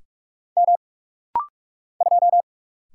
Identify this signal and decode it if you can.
Morse code, heard as a keyed tone, about 46 wpm, 700 Hz. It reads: M 2
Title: Single Letter-Number Alt - Rapid-Fire 45wpm